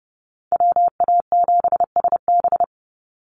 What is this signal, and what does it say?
Morse code, heard as a keyed tone, about 30 wpm, 700 Hz. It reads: WA7SB
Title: NA SOTA Chaser Callsigns 30wpm